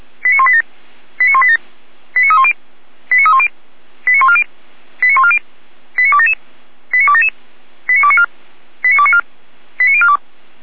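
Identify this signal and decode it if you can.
Five-tone selective call (ZVEI), heard as quick runs of five tones, tones of about 70 ms. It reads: 89187 89187 89310 89310 89140 89140 89277 89277 89294 89294 89042
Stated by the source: ZVEI five-tone selective call received on an SDR.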